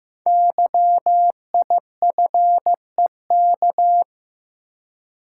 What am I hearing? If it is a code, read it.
Morse code, heard as a keyed tone, about 15 wpm, 700 Hz. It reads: YIFEK
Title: Code Groups - 5 Letters - 15wpm